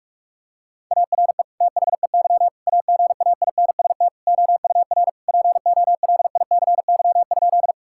Morse code, heard as a keyed tone, about 46 wpm, 700 Hz. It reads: ARE THEY AGAINST OUR POLICY?
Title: Sentences from Top 1000 Words - Review Part 5 - 45wpm